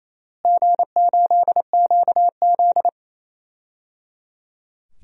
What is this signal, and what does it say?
Morse code, heard as a keyed tone, about 28 wpm, 700 Hz. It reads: G8QZ
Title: DXCC-100 Call Signs - 4 Characters - Repeated 3x - 28wpm